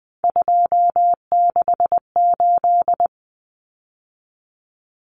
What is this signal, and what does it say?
Morse code, heard as a keyed tone, about 20 wpm, 700 Hz. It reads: 268
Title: Common Cut Numbers - 3-Digit 20wpm